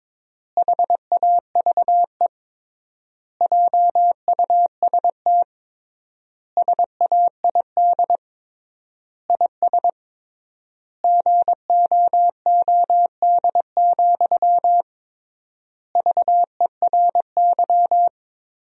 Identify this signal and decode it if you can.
Morse code, heard as a keyed tone, about 22 wpm, 700 Hz. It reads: HAVE JUST SAID IS GOOD, VERY
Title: 3x Word-Spacing - Sentences from Top 200 Words - 22wpm